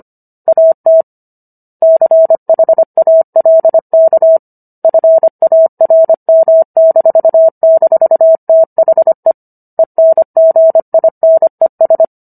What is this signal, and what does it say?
Morse code, heard as a keyed tone, about 25 wpm, 650 Hz. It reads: AT CHALK FARM--THE ENGINES